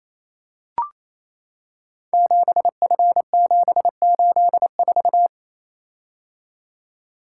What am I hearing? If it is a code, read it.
Morse code, heard as a keyed tone, about 28 wpm, 700 Hz. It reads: 7F784
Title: Code Groups - 5 Letter:Numbers - 28wpm